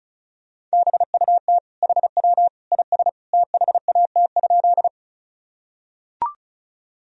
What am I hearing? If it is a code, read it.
Morse code, heard as a keyed tone, about 36 wpm, 700 Hz. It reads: BUT HW IS THAT?
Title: Sentences from Top 100 Words and Abbreviations 35wpm